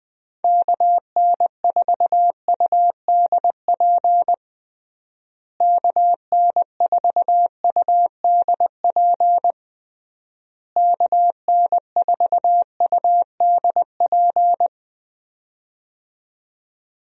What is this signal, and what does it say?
Morse code, heard as a keyed tone, about 21 wpm, 700 Hz. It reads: KN4UDP KN4UDP KN4UDP
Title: USA Call Signs - 6 Characters - Repeated 3x - 20wpm